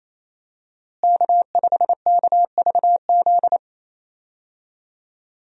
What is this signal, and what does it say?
Morse code, heard as a keyed tone, about 28 wpm, 700 Hz. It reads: K5KVZ
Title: USA Call Signs - 5 Characters - Repeated 3x - 28wpm